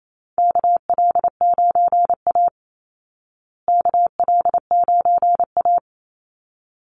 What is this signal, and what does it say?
Morse code, heard as a keyed tone, about 28 wpm, 700 Hz. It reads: KL9A KL9A
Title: Call Signs Repeated 2x - 28wpm